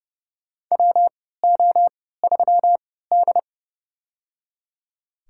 Morse code, heard as a keyed tone, about 30 wpm, spaced as for 17 wpm, 700 Hz. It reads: WO3D
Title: USA Call Signs - 3x Character Spacing - 30wpm